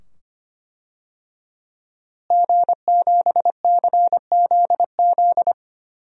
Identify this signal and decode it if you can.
Morse code, heard as a keyed tone, about 25 wpm, 700 Hz. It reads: G7CZZ
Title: DXCC-100 Call Signs - 5 Characters - Repeated 2x - 25wpm